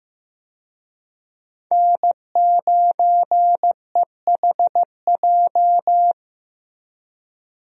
Morse code, heard as a keyed tone, about 15 wpm, 700 Hz. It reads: N9EHJ
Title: USA Call Signs - 5 Characters - Repeated 2x - 15wpm